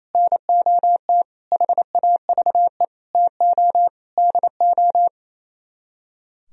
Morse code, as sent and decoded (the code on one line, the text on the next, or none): -. --- - / .... .- ...- . / - --- / -.. ---
NOT HAVE TO DO